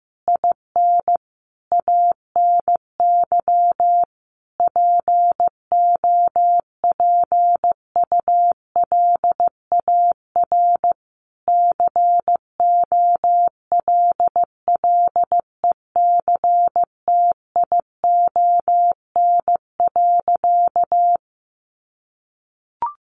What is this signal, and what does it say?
Morse code, heard as a keyed tone, about 15 wpm, 700 Hz. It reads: IN ANY POPULAR COLLECTION.
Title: Sentences from Top 1200 Words 15wpm